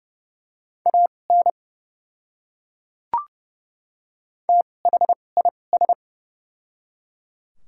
Morse code, heard as a keyed tone, about 30 wpm, spaced as for 22 wpm, 700 Hz. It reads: AN THIS
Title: 2x Character Spacing - Top 100 Words - 30wpm